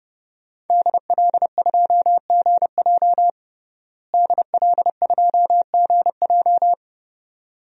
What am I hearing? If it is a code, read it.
Morse code, heard as a keyed tone, about 30 wpm, 700 Hz. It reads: DL2GJ DL2GJ
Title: DXCC-100 Call Signs - 5 Characters - Repeated 2x - 30wpm